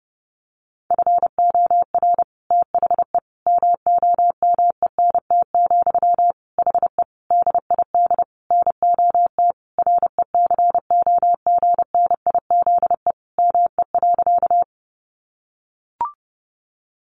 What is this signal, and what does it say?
Morse code, heard as a keyed tone, about 30 wpm, 700 Hz. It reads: FOR THE MOMENT, HE DID NOT RECOGNIZE ME.
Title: Sentences from Top 1400 Words 30wpm